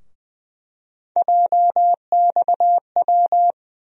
Morse code, heard as a keyed tone, about 20 wpm, 700 Hz. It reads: JXW
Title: USA Call Sign Postfixes - 3 Letters 20wpm